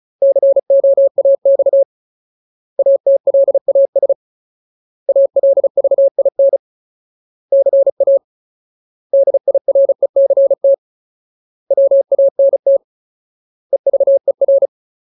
Morse code, heard as a keyed tone, about 34 wpm, 550 Hz. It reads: COAX ATLAS ALVIN CA DIRECT WANT EVER
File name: Ultimate Word List / 35WPM Ultimate